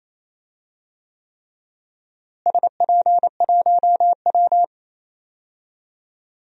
Morse code, heard as a keyed tone, about 28 wpm, 700 Hz. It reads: SP1W
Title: DXCC-100 Call Signs - 4 Characters - Repeated 3x - 28wpm